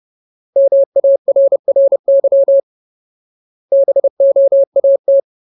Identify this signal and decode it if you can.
Morse code, heard as a keyed tone, about 30 wpm, 550 Hz. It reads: MARRY BOAT
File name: Ultimate Word List / 30WPM Ultimate